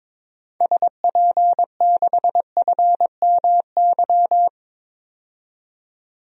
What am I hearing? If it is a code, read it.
Morse code, heard as a keyed tone, about 22 wpm, 700 Hz. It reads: SP6FMY
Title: DXCC-100 Call Signs - 6 Characters - 22wpm